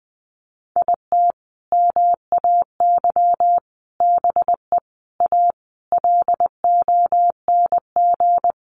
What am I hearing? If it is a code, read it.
Morse code, heard as a keyed tone, about 19 wpm, 700 Hz. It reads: IT MAY BE A LONG
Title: Sentences from Top 1100 Words 20wpm